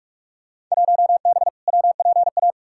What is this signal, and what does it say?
Morse code, heard as a keyed tone, about 43 wpm, 700 Hz. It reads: 1B WPA
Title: Field Day Exchanges Encore - 45wpm